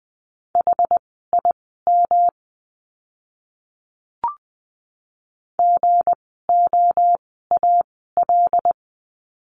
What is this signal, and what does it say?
Morse code, heard as a keyed tone, about 20 wpm, spaced as for 14 wpm, 700 Hz. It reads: HIM GOAL
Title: 160 Lesson 26f - Letter g - Words 20wpm